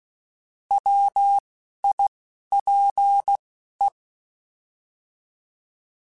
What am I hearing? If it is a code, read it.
Morse code, heard as a keyed tone, about 16 wpm, 800 Hz. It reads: W I P E
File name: Alphabetical / Alphabetical mp3s / Ws @10